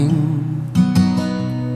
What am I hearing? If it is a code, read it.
There is no code.